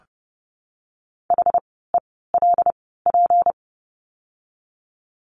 Morse code, heard as a keyed tone, about 30 wpm, spaced as for 17 wpm, 700 Hz. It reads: HELP
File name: course / 30:10wpm / 230 Lesson 37f - Letter x - Words 30wpm